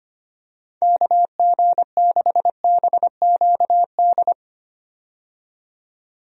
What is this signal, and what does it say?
Morse code, heard as a keyed tone, about 25 wpm, 700 Hz. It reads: KG6BQD
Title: USA Call Signs - 6 Characters - 25wpm